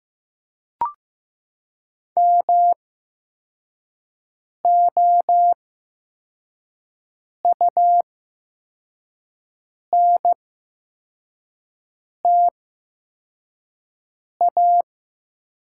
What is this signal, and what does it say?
Morse code, heard as a keyed tone, about 15 wpm, spaced as for 4 wpm, 700 Hz. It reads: MOUNTA
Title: 8x Character Spacing - Top 300 Words - 15wpm